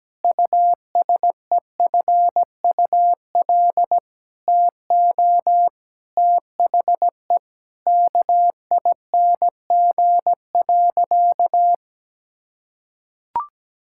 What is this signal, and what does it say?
Morse code, heard as a keyed tone, about 17 wpm, 700 Hz. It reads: USEFUL TO THE KING.
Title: Sentences from Top 1000 Words - Review Part 6 - 17wpm